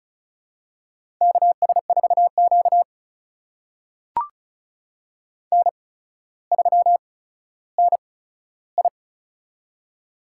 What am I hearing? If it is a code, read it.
Morse code, heard as a keyed tone, about 36 wpm, spaced as for 15 wpm, 700 Hz. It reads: KS4Q N3NI
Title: USA Call Signs - 4 Characters - 8x Character Spacing - 35wpm